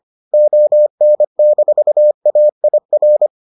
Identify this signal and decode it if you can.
Morse code, heard as a keyed tone, about 25 wpm, 600 Hz. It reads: ON-AIR